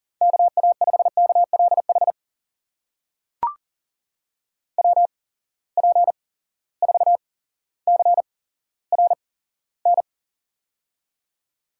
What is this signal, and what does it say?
Morse code, heard as a keyed tone, about 41 wpm, spaced as for 17 wpm, 700 Hz. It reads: KA5KLH WP4CRN
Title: USA Call Signs - 6 Characters - 8x Character Spacing - 40wpm